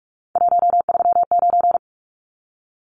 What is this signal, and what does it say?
Morse code, heard as a keyed tone, about 44 wpm, 700 Hz. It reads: 139
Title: Numbers - 3-Digit 45wpm